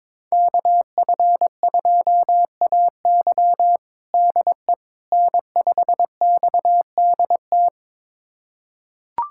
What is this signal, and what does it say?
Morse code, heard as a keyed tone, about 21 wpm, 700 Hz. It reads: KF2AY DE N5XDT
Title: USA Callsign DE Callsign - 5 Characters - 22wpm